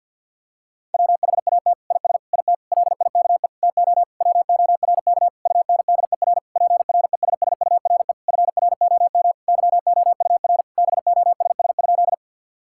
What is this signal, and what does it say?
Morse code, heard as a keyed tone, about 52 wpm, 700 Hz. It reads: WHAT IS IT LIKE TO WORK UNDER PRESSURE FROM YOUR BOSS?
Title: Sentences from Top 1000 Words - Review Part 6 - 50wpm